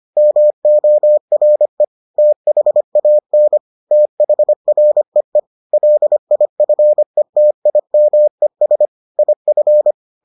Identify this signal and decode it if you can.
Morse code, heard as a keyed tone, about 25 wpm, 600 Hz. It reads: MORE THAN THREE LIFETIMES IF